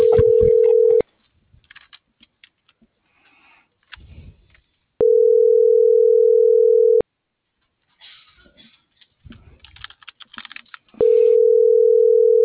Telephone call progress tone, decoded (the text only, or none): ringback tone